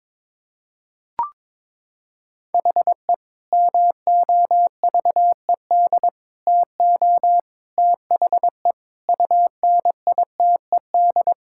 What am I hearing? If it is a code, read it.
Morse code, heard as a keyed tone, about 21 wpm, 700 Hz. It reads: HE MOVED TO THE UNITED
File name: practice / 22wpm / Sentences with 3-Word Phrases - 22wpm